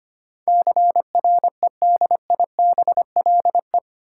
Morse code, heard as a keyed tone, about 25 wpm, 700 Hz. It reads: CREDIBLE
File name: Eight Letter Words - 25wpm